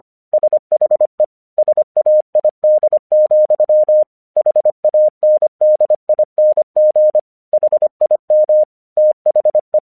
Morse code, heard as a keyed tone, about 24 wpm, 610 Hz. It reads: SHE SAID, HANDING HIM THE
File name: 1 English 25wpm / AQC19